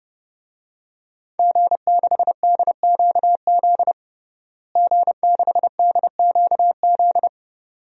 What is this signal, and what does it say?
Morse code, heard as a keyed tone, about 30 wpm, 700 Hz. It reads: G6DQZ G6DQZ
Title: DXCC-100 Call Signs - 5 Characters - Repeated 2x - 30wpm